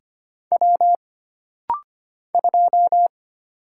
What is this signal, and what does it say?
Morse code, heard as a keyed tone, about 25 wpm, 700 Hz. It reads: W 2
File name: Single Character (Letter:Number:Punctuation:Prosign) - Rapid-Fire - 25wpm